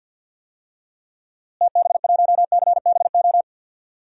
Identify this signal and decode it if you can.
Morse code, heard as a keyed tone, about 50 wpm, 700 Hz. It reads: TB1CBO
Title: DXCC-100 Call Signs - 6 Characters - Repeated 3x - 50wpm